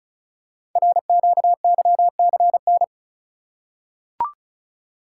Morse code, heard as a keyed tone, about 36 wpm, 700 Hz. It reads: RQYCN
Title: Code Groups - 5 Letters - 35wpm